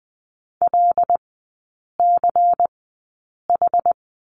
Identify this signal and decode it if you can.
Morse code, heard as a keyed tone, about 20 wpm, spaced as for 8 wpm, 690 Hz. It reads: LCH
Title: 198 Lesson 32e - Letter k - Three Characters 20wpm